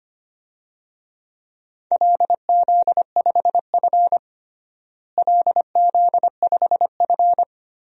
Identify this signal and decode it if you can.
Morse code, heard as a keyed tone, about 25 wpm, 700 Hz. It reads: LZ5F LZ5F
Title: DXCC-100 Call Signs - 4 Characters - Repeated 3x - 25wpm